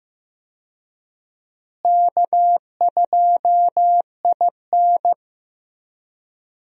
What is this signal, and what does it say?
Morse code, heard as a keyed tone, about 15 wpm, 700 Hz. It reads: K2IN
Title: USA Call Signs - 4 Characters - Repeated 2x - 15wpm